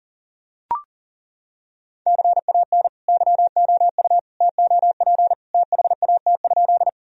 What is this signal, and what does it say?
Morse code, heard as a keyed tone, about 40 wpm, 700 Hz. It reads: CAN YOU TOP THAT?
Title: Sentences from Top 1000 Words - Review Part 2 - 40wpm